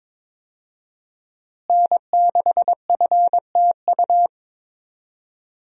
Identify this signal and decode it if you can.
Morse code, heard as a keyed tone, about 22 wpm, 700 Hz. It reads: N6FTU
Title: USA Call Signs - 5 Characters - Repeated 3x - 22wpm